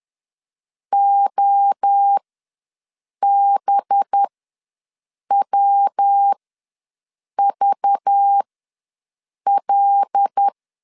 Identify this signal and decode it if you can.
Morse code, heard as a keyed tone, about 10 wpm, spaced as for 6 wpm, 800 Hz. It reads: OBWVL
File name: LEZIONI / 5 LBVC / 5 LBV ws 10wpm 3645